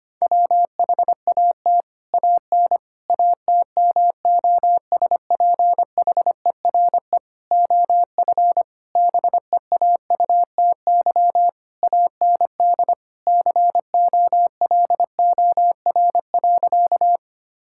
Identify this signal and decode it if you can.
Morse code, heard as a keyed tone, about 25 wpm, 700 Hz. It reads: WHAT AN ATMOSPHERE OF BEAUTY AND COLOR.
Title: Sentences from Top 1600 Words 25wpm